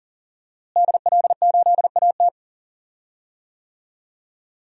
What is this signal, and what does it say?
Morse code, heard as a keyed tone, about 40 wpm, 700 Hz. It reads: DL8AT
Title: DXCC-100 Call Signs - 5 Characters - Repeated 3x - 40wpm